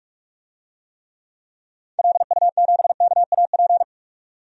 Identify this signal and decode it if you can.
Morse code, heard as a keyed tone, about 46 wpm, 700 Hz. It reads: LU7KAP
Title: DXCC-100 Call Signs - 6 Characters - 45wpm